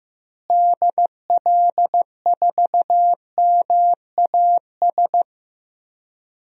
Morse code, heard as a keyed tone, about 15 wpm, 700 Hz. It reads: DL4MAS
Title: DXCC-100 Call Signs - 6 Characters - Repeated 3x - 15wpm